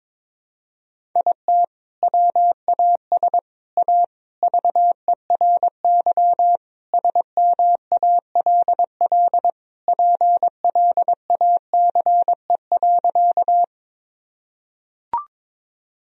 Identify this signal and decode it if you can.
Morse code, heard as a keyed tone, about 22 wpm, 700 Hz. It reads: IT WAS A VERY SMALL PLACE.
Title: Sentences from Top 1000 Words - Review Part 1 - 22wpm